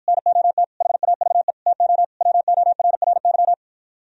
Morse code, heard as a keyed tone, about 52 wpm, 700 Hz. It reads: NOT HAVE TO WORRY